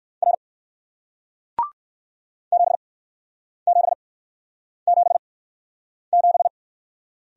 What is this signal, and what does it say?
Morse code, heard as a keyed tone, about 61 wpm, spaced as for 9 wpm, 700 Hz. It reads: R 7777